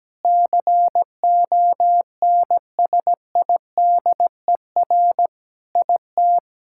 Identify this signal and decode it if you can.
Morse code, heard as a keyed tone, about 17 wpm, 700 Hz. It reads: CONSIDER IT
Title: Sentences from Top 1000 Words - Review Part 6 - 17wpm